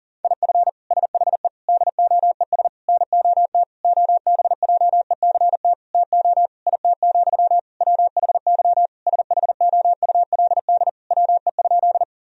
Morse code, heard as a keyed tone, about 40 wpm, 700 Hz. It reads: IF SHE DOES NOT OBJECT TO IT, WHY SHOULD WE?
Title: Sentences from Top 1000 Words - Review Part 3 - 40wpm